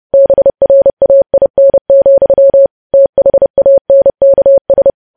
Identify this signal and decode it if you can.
Morse code, heard as a keyed tone, about 30 wpm, 550 Hz. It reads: BRAIN, THANKS
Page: news headlines in morse code at 30 wpm